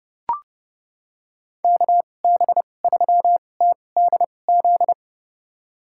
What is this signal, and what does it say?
Morse code, heard as a keyed tone, about 31 wpm, spaced as for 22 wpm, 700 Hz. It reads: KB3TDZ
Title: USA Call Signs - 6 Characters - 2x Character Spacing - 30wpm